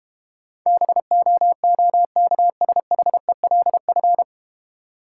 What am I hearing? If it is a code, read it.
Morse code, heard as a keyed tone, about 31 wpm, 700 Hz. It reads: BOOKSHELF
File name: Easy Words - Speed-Racing with Hints - 25wpm